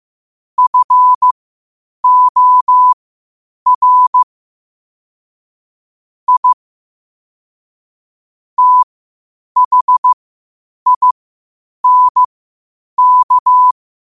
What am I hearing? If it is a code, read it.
Morse code, heard as a keyed tone, about 15 wpm, spaced as for 8 wpm, 1000 Hz. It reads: FOR I THINK
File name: qotdfiles / 8wpm